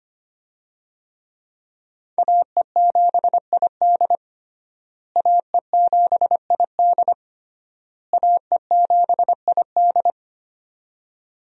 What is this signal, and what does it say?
Morse code, heard as a keyed tone, about 25 wpm, 700 Hz. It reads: AE7ID AE7ID AE7ID
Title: USA Call Signs - 5 Characters - Repeated 3x - 25wpm